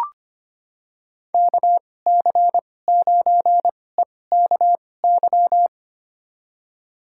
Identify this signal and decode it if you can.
Morse code, heard as a keyed tone, about 25 wpm, spaced as for 18 wpm, 700 Hz. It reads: KC9EKY
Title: USA Call Signs - 6 Characters - 2x Character Spacing - 25wpm